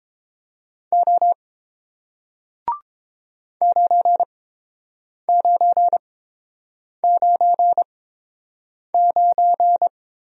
Morse code, heard as a keyed tone, about 30 wpm, spaced as for 7 wpm, 690 Hz. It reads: O 9999